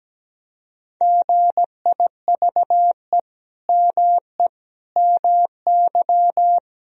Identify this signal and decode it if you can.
Morse code, heard as a keyed tone, about 17 wpm, 700 Hz. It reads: GIVE ME MY
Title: Sentences from Top 1000 Words - Review Part 3 - 17wpm